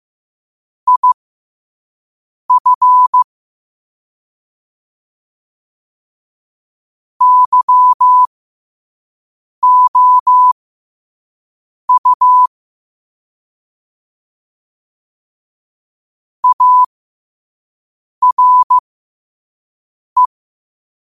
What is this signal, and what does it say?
Morse code, heard as a keyed tone, about 15 wpm, spaced as for 5 wpm, 1000 Hz. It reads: IF YOU ARE